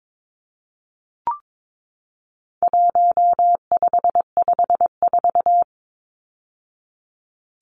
Morse code, heard as a keyed tone, about 22 wpm, 700 Hz. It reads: 1554
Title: Numbers - 4-Digit 22wpm